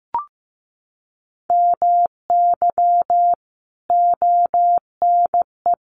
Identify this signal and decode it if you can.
Morse code, heard as a keyed tone, about 15 wpm, 700 Hz. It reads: MY ONE
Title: Sentences from Top 1200 Words 15wpm